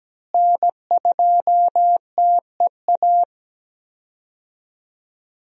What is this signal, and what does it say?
Morse code, heard as a keyed tone, about 17 wpm, 700 Hz. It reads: N2TEA
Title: USA Call Signs - 5 Characters - Repeated 2x - 17wpm